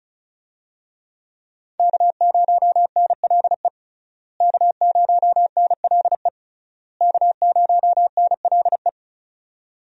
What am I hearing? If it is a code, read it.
Morse code, heard as a keyed tone, about 36 wpm, 700 Hz. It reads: K0NLE K0NLE K0NLE
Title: USA Call Signs - 5 Characters - Repeated 3x - 35wpm